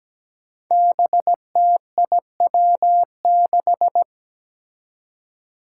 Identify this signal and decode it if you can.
Morse code, heard as a keyed tone, about 17 wpm, 700 Hz. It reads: BTIW6
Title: Five Challenging Letter-Number Combinations 17wpm